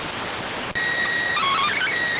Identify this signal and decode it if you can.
SSTV (slow-scan television), PD 290.